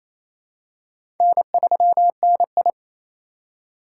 Morse code, heard as a keyed tone, about 28 wpm, 700 Hz. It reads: N3NI
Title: USA Call Signs - 4 Characters - 4x Character Spacing - 28wpm